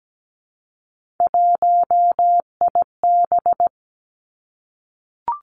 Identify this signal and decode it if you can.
Morse code, heard as a keyed tone, about 17 wpm, 700 Hz. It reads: 1IB
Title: Three Challenging Letter-Number Combinations 17wpm